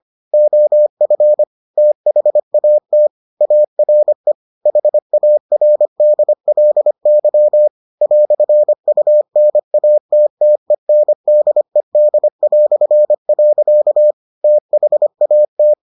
Morse code, heard as a keyed tone, about 25 wpm, 600 Hz. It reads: OF THAT ARE HARDLY "UNATTENDED". THAT